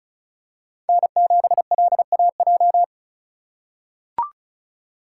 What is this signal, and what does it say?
Morse code, heard as a keyed tone, about 36 wpm, 700 Hz. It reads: N7LAJ
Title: USA Call Signs - 5 Characters - 3x Character Spacing - 35wpm